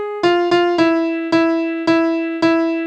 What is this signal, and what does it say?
There is no code.